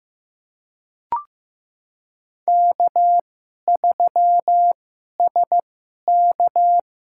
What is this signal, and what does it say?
Morse code, heard as a keyed tone, about 15 wpm, spaced as for 11 wpm, 700 Hz. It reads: K3SK
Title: USA Call Signs - 2x Character Spacing - 15wpm